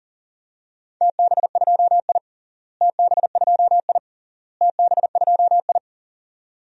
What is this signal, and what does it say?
Morse code, heard as a keyed tone, about 40 wpm, 700 Hz. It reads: TB2I TB2I TB2I
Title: DXCC-100 Call Signs - 4 Characters - Repeated 3x - 40wpm